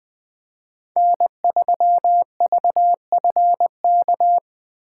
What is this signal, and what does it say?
Morse code, heard as a keyed tone, about 20 wpm, 700 Hz. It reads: N3VFK
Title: Speed Racing - USA 1x3 Call Signs 17wpm